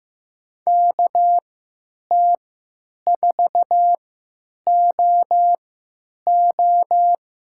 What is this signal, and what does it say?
Morse code, heard as a keyed tone, about 15 wpm, spaced as for 9 wpm, 700 Hz. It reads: KT4OO
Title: USA Call Signs - 5 Characters - 3x Character Spacing - 15wpm